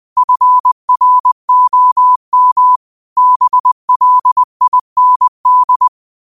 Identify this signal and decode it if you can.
Morse code, heard as a keyed tone, about 20 wpm, 1000 Hz. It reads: FROM BLIND
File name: qotdfiles / 20wpm